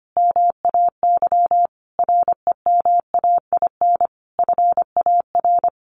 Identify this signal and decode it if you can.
Morse code, heard as a keyed tone, about 25 wpm, 700 Hz. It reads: MAY REMAIN FAR
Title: Sentences from Top 1400 Words 25wpm